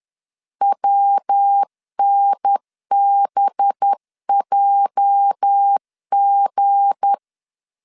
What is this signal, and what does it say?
Morse code, heard as a keyed tone, about 10 wpm, 800 Hz. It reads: WNBJG
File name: LEZIONI / 9 2 10wpm 2736